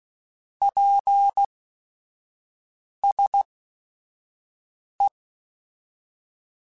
Morse code, heard as a keyed tone, about 16 wpm, spaced as for 5 wpm, 800 Hz. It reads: PSE
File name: QSOs / QSOs-mp3s / kc7sac@5